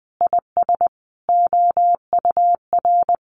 Morse code, heard as a keyed tone, about 20 wpm, 700 Hz. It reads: IS OUR
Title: Sentences from Top 1500 Words 20wpm